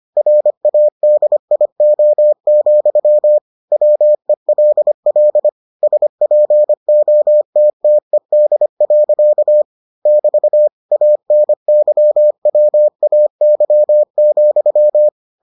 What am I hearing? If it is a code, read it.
Morse code, heard as a keyed tone, about 25 wpm, 600 Hz. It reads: RADIO, WELL SPOTTED. <BT> ANYWAY,